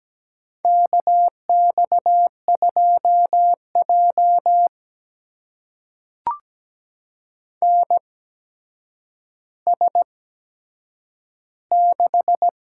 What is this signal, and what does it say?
Morse code, heard as a keyed tone, about 17 wpm, spaced as for 8 wpm, 700 Hz. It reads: KX2J NS6
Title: USA Call Signs - 4 Characters - 8x Character Spacing - 17wpm